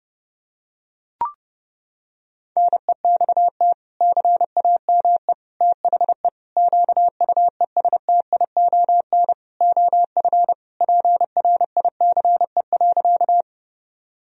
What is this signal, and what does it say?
Morse code, heard as a keyed tone, about 28 wpm, 700 Hz. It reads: NEXT CAME THE QUESTION OF PRICE.
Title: Sentences from Top 1000 Words - Review Part 4 - 30wpm